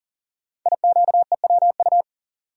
Morse code, heard as a keyed tone, about 39 wpm, 700 Hz. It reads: IQEWU
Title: Code Groups - 5 Letters - 40wpm